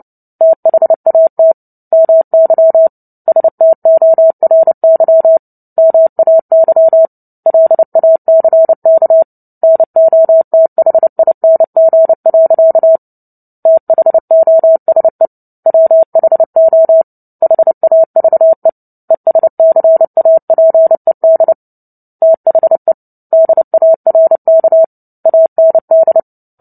Morse code, heard as a keyed tone, about 30 wpm, 650 Hz. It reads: THAT MY STORY MAY LACK NOTHING. THOSE WHO HAVE ESCAPED THE DARK AND
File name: Book Two / 03-Chapter 3 - 29 WPM